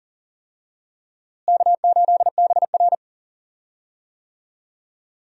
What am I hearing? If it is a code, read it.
Morse code, heard as a keyed tone, about 40 wpm, 700 Hz. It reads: K8BR
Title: USA Call Signs - 4 Characters - Repeated 3x - 40wpm